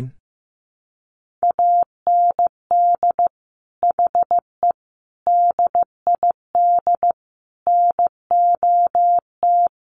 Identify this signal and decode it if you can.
Morse code, heard as a keyed tone, about 14 wpm, 700 Hz. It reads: AND HE DID NOT
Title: Sentences from Top 1300 Words 15wpm